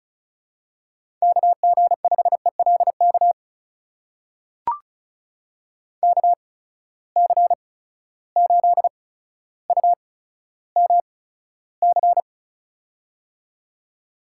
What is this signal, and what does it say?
Morse code, heard as a keyed tone, about 35 wpm, spaced as for 15 wpm, 700 Hz. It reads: KG5ELK KC8UMC
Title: USA Call Signs - 6 Characters - 8x Character Spacing - 35wpm